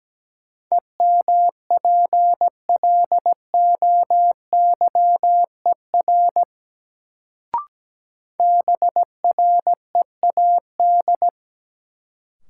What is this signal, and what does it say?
Morse code, heard as a keyed tone, about 17 wpm, 700 Hz. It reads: EMPLOYER BREAD
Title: Sentences from Top 1600 Words - New Word Drill Alt - 17wpm